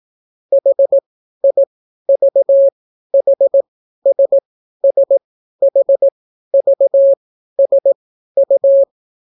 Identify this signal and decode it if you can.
Morse code, heard as a keyed tone, about 18 wpm, spaced as for 12 wpm, 550 Hz. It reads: HIVHSSHVSU